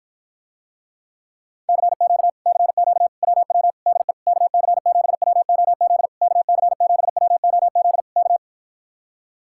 Morse code, heard as a keyed tone, about 53 wpm, 700 Hz. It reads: CQ CQ WW DE KC7WOZ KC7WOZ K